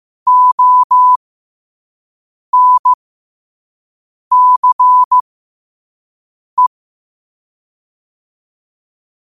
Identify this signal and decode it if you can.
Morse code, heard as a keyed tone, about 15 wpm, spaced as for 5 wpm, 1000 Hz. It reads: ONCE
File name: qotdfiles / 5wpm